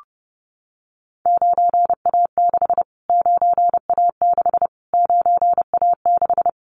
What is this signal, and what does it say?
Morse code, heard as a keyed tone, about 30 wpm, 700 Hz. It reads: 9A6 9A6 9A6
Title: Callsign Prefixes for Top 25 DXCC - Repeated 3x - 30wpm